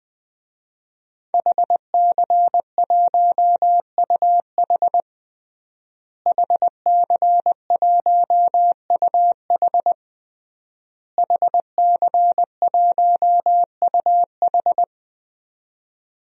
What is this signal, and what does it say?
Morse code, heard as a keyed tone, about 20 wpm, 700 Hz. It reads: HC1UH HC1UH HC1UH
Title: DXCC-100 Call Signs - 5 Characters - Repeated 3x - 20wpm